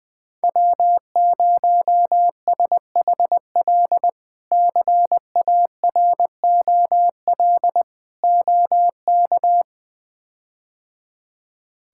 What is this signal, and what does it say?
Morse code, heard as a keyed tone, about 20 wpm, 700 Hz. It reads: W0SHL CAROL OK
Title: K1USN SST QSO Practice - 20wpm